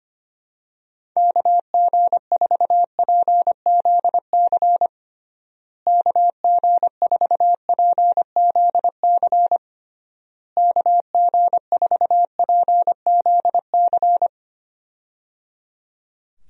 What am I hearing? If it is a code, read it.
Morse code, heard as a keyed tone, about 25 wpm, 700 Hz. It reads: KG4PZC KG4PZC KG4PZC
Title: USA Call Signs - 6 Characters - Repeated 3x - 25wpm